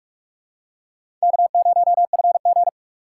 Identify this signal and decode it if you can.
Morse code, heard as a keyed tone, about 46 wpm, 700 Hz. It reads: K0FG
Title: USA Call Signs - 4 Characters - 4x Character Spacing - 45wpm